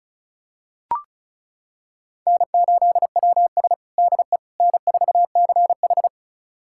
Morse code, heard as a keyed tone, about 36 wpm, 700 Hz. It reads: N8WS DE N4CH
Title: USA Callsign DE Callsign - 4 Characters - 35wpm